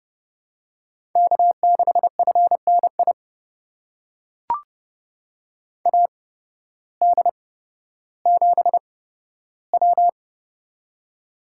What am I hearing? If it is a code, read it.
Morse code, heard as a keyed tone, about 30 wpm, spaced as for 14 wpm, 700 Hz. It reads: K6FNI AD7W
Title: USA Call Signs - 8x Character Spacing - 30wpm